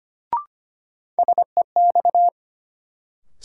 Morse code, heard as a keyed tone, about 25 wpm, 700 Hz. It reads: SEX
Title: Sentences from Top 1600 Words - New Word Drill Alt - 25wpm